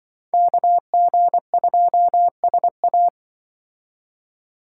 Morse code, heard as a keyed tone, about 24 wpm, 700 Hz. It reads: KG2SA
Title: Speed Racing - USA 2x2 Call Signs 20wpm